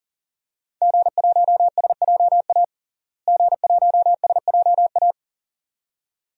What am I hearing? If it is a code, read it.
Morse code, heard as a keyed tone, about 40 wpm, 700 Hz. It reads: G1SJA G1SJA
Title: DXCC-100 Call Signs - 5 Characters - Repeated 2x - 40wpm